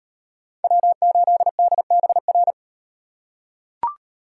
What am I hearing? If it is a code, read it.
Morse code, heard as a keyed tone, about 38 wpm, 700 Hz. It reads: W8DBR